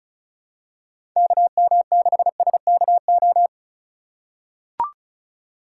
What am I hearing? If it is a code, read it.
Morse code, heard as a keyed tone, about 36 wpm, 700 Hz. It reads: KM6SKO
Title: USA Call Signs - 8x Character Spacing - 35wpm